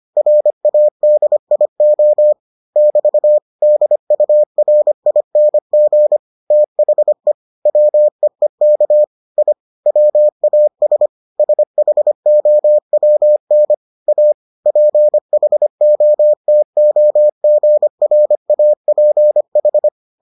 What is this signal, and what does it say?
Morse code, heard as a keyed tone, about 25 wpm, 600 Hz. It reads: RADIO <BT> DURING THE WEEK I WAS SHOWN A PHOTOGRAPH